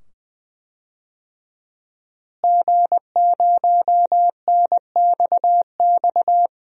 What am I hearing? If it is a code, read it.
Morse code, heard as a keyed tone, about 20 wpm, 700 Hz. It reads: G0NXX
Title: DXCC-100 Call Signs - 5 Characters - Repeated 2x - 20wpm